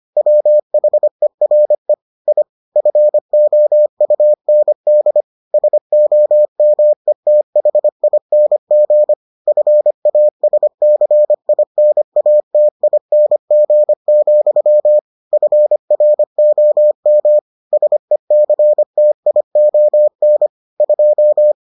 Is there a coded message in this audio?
Yes: Morse code from 0.2 to 21.5 s, heard as a keyed tone, about 25 wpm, 600 Hz. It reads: WHERE I FOUND SOMETHING FASCINATING, FROM SECTION 2